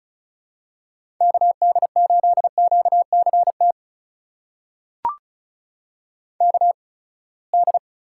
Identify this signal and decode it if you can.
Morse code, heard as a keyed tone, about 36 wpm, spaced as for 24 wpm, 700 Hz. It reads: KD8QCT KD
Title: USA Call Signs - 6 Characters - 8x Character Spacing - 35wpm